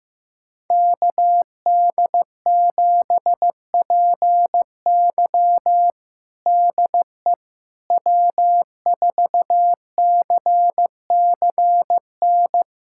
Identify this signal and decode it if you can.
Morse code, heard as a keyed tone, about 15 wpm, 700 Hz. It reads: KD7PY DE W4CCN
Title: USA Callsign DE Callsign - 5 Characters - 15wpm